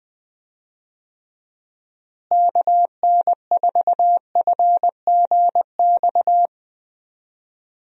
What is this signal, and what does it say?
Morse code, heard as a keyed tone, about 19 wpm, 700 Hz. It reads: KN4FGX
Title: USA Call Signs - 6 Characters - Repeated 2x - 20wpm